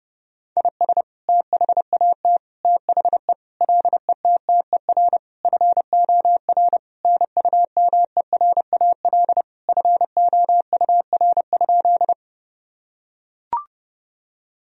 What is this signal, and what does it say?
Morse code, heard as a keyed tone, about 30 wpm, 700 Hz. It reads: IS THAT THE LETTER FOR NUMERAL FOUR?
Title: Sentences from Top 1000 Words - Review Part 2 - 30wpm